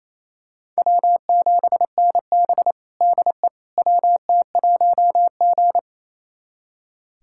Morse code, heard as a keyed tone, about 28 wpm, 700 Hz. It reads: W7NB DE WT1G
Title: USA Callsign DE Callsign - 4 Characters - 28wpm